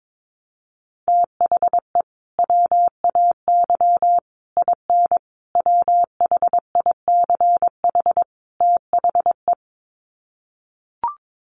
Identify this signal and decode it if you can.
Morse code, heard as a keyed tone, about 22 wpm, 700 Hz. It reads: THE WAY IN WHICH THE
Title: Sets of 5 Words Encore2 22wpm